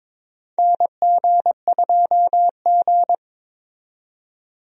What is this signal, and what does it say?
Morse code, heard as a keyed tone, about 22 wpm, 700 Hz. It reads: NG2G
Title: USA Call Signs - 4 Characters - 22wpm